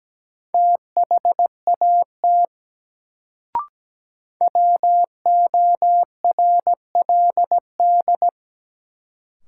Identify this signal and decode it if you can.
Morse code, heard as a keyed tone, about 17 wpm, 700 Hz. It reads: THAT WORLD
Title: Sets of 2 Words - New Word Drill - 17wpm